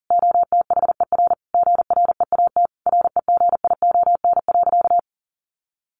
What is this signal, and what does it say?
Morse code, heard as a keyed tone, about 40 wpm, 700 Hz. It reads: OTHER GREAT REGION.
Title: Sentences from Top 1500 Words 40wpm